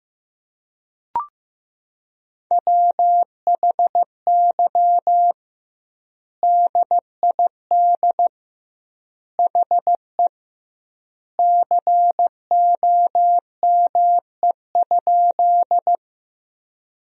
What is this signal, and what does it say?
Morse code, heard as a keyed tone, about 16 wpm, 700 Hz. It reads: WHY DID HE COME?
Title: 2x Word-Spacing - Sentences from Top 200 Words - 15wpm